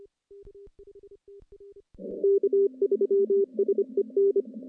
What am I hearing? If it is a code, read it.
Morse code, heard as a keyed tone, about 25 wpm, 400 Hz. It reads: K3SEN
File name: contests / CW-SS-k5tr- -1200